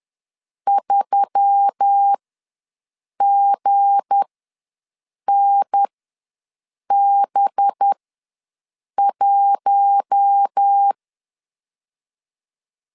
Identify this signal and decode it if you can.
Morse code, heard as a keyed tone, about 10 wpm, spaced as for 6 wpm, 800 Hz. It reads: 3GNB1
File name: LEZIONI / 9 2 ws 10wpm 4130